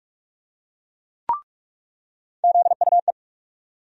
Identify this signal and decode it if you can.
Morse code, heard as a keyed tone, about 46 wpm, 700 Hz. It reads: ZUE
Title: USA Call Sign Postfixes - 3 Letters 45wpm